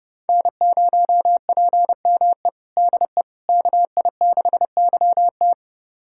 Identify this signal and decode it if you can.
Morse code, heard as a keyed tone, about 30 wpm, 700 Hz. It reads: N0PME DE KI6YT
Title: USA Callsign DE Callsign - 5 Characters - 30wpm